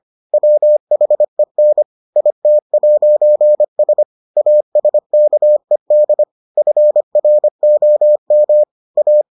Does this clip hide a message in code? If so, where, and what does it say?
Yes: Morse code from 0.3 to 9.2 s, heard as a keyed tone, about 25 wpm, 600 Hz. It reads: WHEN IT'S ASKED FROM A